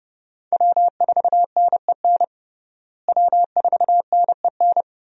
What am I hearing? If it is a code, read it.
Morse code, heard as a keyed tone, about 30 wpm, 700 Hz. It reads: W4NEN W4NEN
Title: USA Call Signs - 5 Characters - Repeated 2x - 30wpm